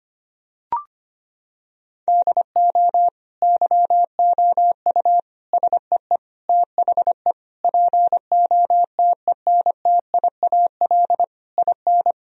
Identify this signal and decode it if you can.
Morse code, heard as a keyed tone, about 25 wpm, 700 Hz. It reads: DO YOU SEE THE POTENTIAL IN
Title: Sentences from Top 1000 Words - Review Part 4 - 25wpm